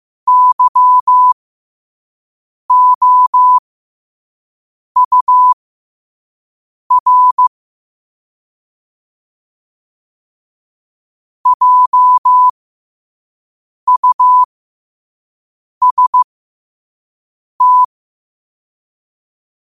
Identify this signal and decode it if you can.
Morse code, heard as a keyed tone, about 15 wpm, spaced as for 5 wpm, 1000 Hz. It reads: YOUR JUST
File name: qotdfiles / 5wpm